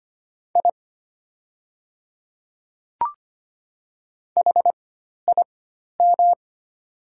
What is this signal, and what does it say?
Morse code, heard as a keyed tone, about 25 wpm, spaced as for 12 wpm, 700 Hz. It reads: I HIM
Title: 4x Character Spacing - Top 100 Words - 25wpm